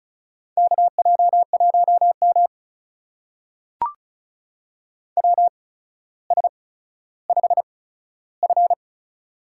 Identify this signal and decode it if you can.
Morse code, heard as a keyed tone, about 36 wpm, spaced as for 15 wpm, 700 Hz. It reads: KJ1M WS5F
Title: USA Call Signs - 4 Characters - 8x Character Spacing - 35wpm